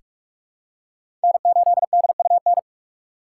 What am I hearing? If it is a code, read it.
Morse code, heard as a keyed tone, about 45 wpm, 700 Hz. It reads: N8DUN
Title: USA Call Signs - 5 Characters - 8x Character Spacing - 45wpm